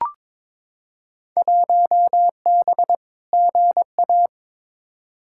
Morse code, heard as a keyed tone, about 23 wpm, 700 Hz. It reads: 1B GA
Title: Field Day Exchanges - 22wpm